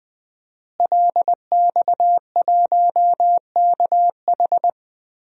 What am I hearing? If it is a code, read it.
Morse code, heard as a keyed tone, about 21 wpm, 700 Hz. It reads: LX1KH
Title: DXCC-100 Call Signs - 5 Characters - Repeated 3x - 20wpm